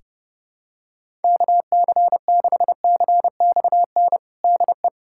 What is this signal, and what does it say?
Morse code, heard as a keyed tone, about 32 wpm, 700 Hz. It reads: KC6CXN DE
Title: USA Callsign DE Callsign - 6 Characters - 30wpm